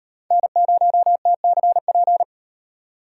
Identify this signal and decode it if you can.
Morse code, heard as a keyed tone, about 38 wpm, 700 Hz. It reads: N0TCP